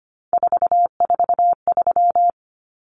Morse code, heard as a keyed tone, about 25 wpm, 700 Hz. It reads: 443
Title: Common Cut Numbers - 3-Digit 25wpm